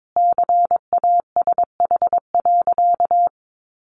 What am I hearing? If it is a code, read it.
Morse code, heard as a keyed tone, about 22 wpm, 700 Hz. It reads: CASH.
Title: Sentences from Top 1300 Words 22wpm